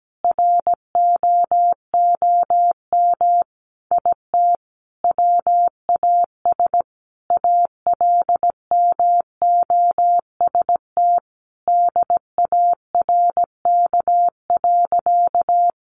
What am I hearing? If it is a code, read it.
Morse code, heard as a keyed tone, about 17 wpm, 700 Hz. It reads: ROOM IT WAS ALMOST DARK.
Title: Sentences from Top 700 Words 17wpm